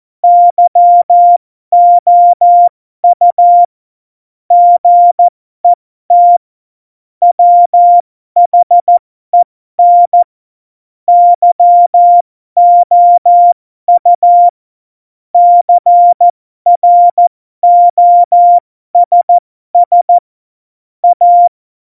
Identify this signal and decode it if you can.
Morse code, heard as a keyed tone, about 14 wpm, spaced as for 12 wpm, 700 Hz. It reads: YOU GET WHEN YOU CROSS A
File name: Morsekurs / Witze